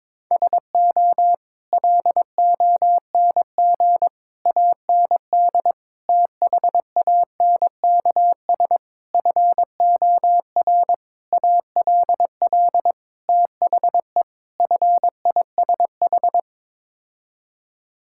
Morse code, heard as a keyed tone, about 22 wpm, 700 Hz. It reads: SO LONG AND THANKS FOR ALL THE FISH
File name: practice / 22wpm / Sets of 8 Words - 22wpm